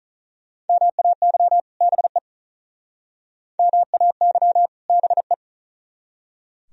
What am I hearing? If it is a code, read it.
Morse code, heard as a keyed tone, about 46 wpm, 700 Hz. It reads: MAY BE MAY BE